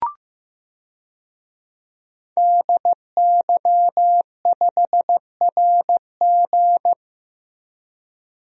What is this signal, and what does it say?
Morse code, heard as a keyed tone, about 15 wpm, 700 Hz. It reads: DY5RG
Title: DXCC-100 Call Signs - 5 Characters - Repeated 2x - 15wpm